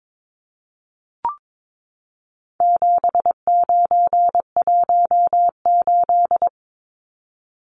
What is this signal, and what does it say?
Morse code, heard as a keyed tone, about 22 wpm, 700 Hz. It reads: 7918
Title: Numbers - 4-Digit 22wpm